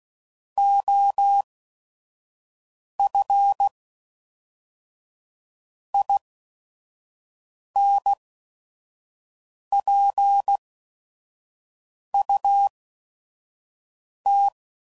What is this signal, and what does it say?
Morse code, heard as a keyed tone, about 16 wpm, spaced as for 5 wpm, 800 Hz. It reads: OF INPUT
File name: CW Code Groups @5WPM